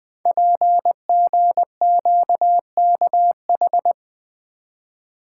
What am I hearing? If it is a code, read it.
Morse code, heard as a keyed tone, about 20 wpm, 700 Hz. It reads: PGQKH
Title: Code Groups - 5 Letters - 20wpm